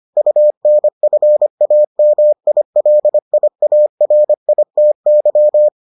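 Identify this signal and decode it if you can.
Morse code, heard as a keyed tone, about 25 wpm, 600 Hz. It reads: UNFAMILIARITY
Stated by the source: Morse edition of the Foundations of Amateur Radio episodes.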